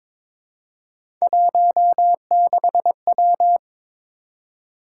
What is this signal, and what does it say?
Morse code, heard as a keyed tone, about 22 wpm, 700 Hz. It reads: 16W
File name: practice / 22wpm / Three Letter-Number Combinations - 22wpm